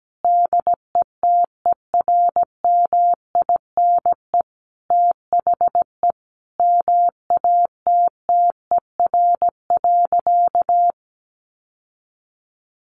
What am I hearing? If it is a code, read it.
Morse code, heard as a keyed tone, about 17 wpm, 700 Hz. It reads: DETERMINE THE MATTER.
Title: Sentences from Top 1300 Words 17wpm